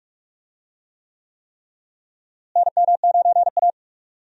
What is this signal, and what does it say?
Morse code, heard as a keyed tone, about 42 wpm, 700 Hz. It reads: NM9A
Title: USA Call Signs - 4 Characters - Repeated 3x - 45wpm